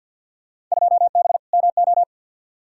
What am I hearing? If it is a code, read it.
Morse code, heard as a keyed tone, about 52 wpm, 700 Hz. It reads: 2B MO
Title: Field Day Exchanges - 50wpm